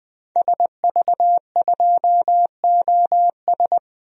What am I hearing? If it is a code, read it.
Morse code, heard as a keyed tone, about 20 wpm, 700 Hz. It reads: SV2OS
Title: DXCC-100 Call Signs - 5 Characters - 20wpm